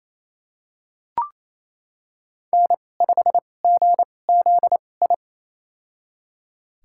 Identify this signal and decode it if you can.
Morse code, heard as a keyed tone, about 27 wpm, spaced as for 20 wpm, 700 Hz. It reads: N5GZI
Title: USA Call Signs - 5 Characters - 2x Character Spacing - 28wpm